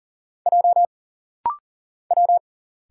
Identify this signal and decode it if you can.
Morse code, heard as a keyed tone, about 40 wpm, 700 Hz. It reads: J W
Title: Single Letter-Number Alt - Rapid-Fire 40wpm